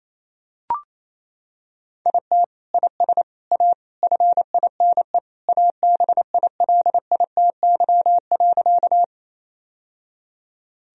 Morse code, heard as a keyed tone, about 28 wpm, 700 Hz. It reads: IT IS A FINE ABILITY.
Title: Sentences from Top 1000 Words - Review Part 3 - 28wpm